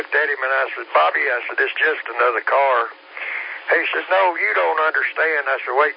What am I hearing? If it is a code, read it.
Emergency Alert System (SAME) tones (header bursts), heard as harsh buzzing bursts, with speech. It is heard from 1.2 to 5.5 s.